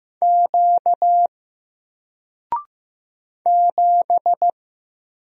Morse code, heard as a keyed tone, about 15 wpm, 700 Hz. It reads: Q 7
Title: Single Character (Letter:Number:Punctuation:Prosign) 15wpm